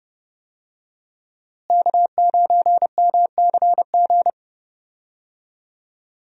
Morse code, heard as a keyed tone, about 29 wpm, 700 Hz. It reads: K9MCG
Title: USA Call Signs - 5 Characters - Repeated 2x - 30wpm